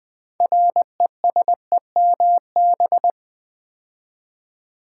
Morse code, heard as a keyed tone, about 20 wpm, 700 Hz. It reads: RESEMB
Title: Build A Word - 2 Letters At A Time - 8-Letter Words - 20wpm